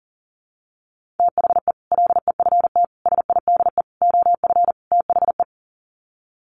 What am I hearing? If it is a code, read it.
Morse code, heard as a keyed tone, about 40 wpm, 700 Hz. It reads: THE LEFT SIDE OF THE
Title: Sets of 5 Words Encore 40wpm